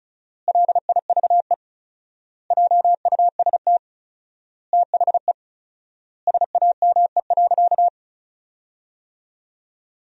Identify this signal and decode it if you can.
Morse code, heard as a keyed tone, about 34 wpm, 700 Hz. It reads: LIVE JUST THE SAME.
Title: 4x Word-Spacing - Sentences from Top 300 Words - 35wpm